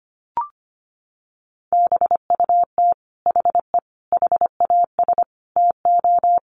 Morse code, heard as a keyed tone, about 25 wpm, 700 Hz. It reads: BUT HE HAS TO
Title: Sentences from Top 600 Words 25wpm